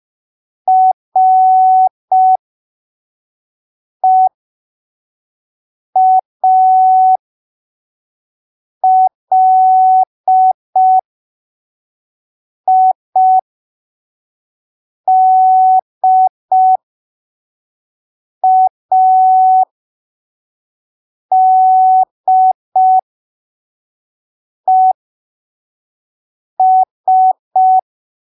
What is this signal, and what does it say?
Morse code, heard as a keyed tone, about 5 wpm, spaced as for 3 wpm, 750 Hz. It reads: REALIDADES
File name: arquivos / texto / texto 15 05ppm